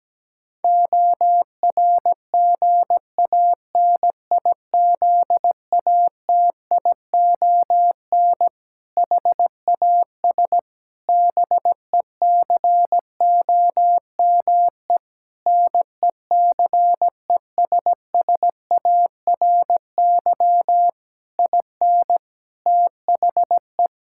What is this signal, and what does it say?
Morse code, heard as a keyed tone, about 17 wpm, 700 Hz. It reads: ORGANIZATION HAS BECOME NECESSARY IN THE
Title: Sentences from Top 1000 Words - Review Part 5 - 17wpm